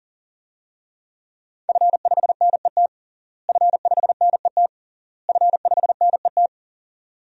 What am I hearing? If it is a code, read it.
Morse code, heard as a keyed tone, about 40 wpm, 700 Hz. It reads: F5NET F5NET F5NET
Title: DXCC-100 Call Signs - 5 Characters - Repeated 3x - 40wpm